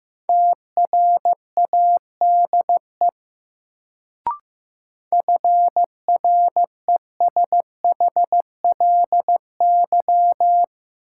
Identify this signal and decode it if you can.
Morse code, heard as a keyed tone, about 15 wpm, 700 Hz. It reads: TRADE FRESHLY
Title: Sets of 5 Words Encore - New Word Drill - 15wpm